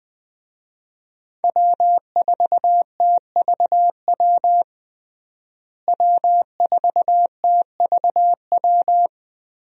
Morse code, heard as a keyed tone, about 20 wpm, 700 Hz. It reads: W4TVW W4TVW
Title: USA Call Signs - 5 Characters - Repeated 3x - 20wpm